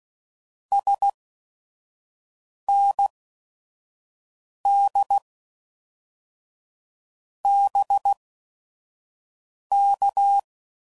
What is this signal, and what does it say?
Morse code, heard as a keyed tone, about 16 wpm, spaced as for 5 wpm, 800 Hz. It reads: SND BK
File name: CW Code Groups @5WPM / QSOs / QSO WA7QOX 1